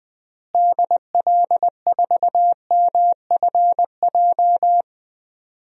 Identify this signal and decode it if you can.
Morse code, heard as a keyed tone, about 20 wpm, 700 Hz. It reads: DL4MFJ
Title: DXCC-100 Call Signs - 6 Characters - Repeated 3x - 20wpm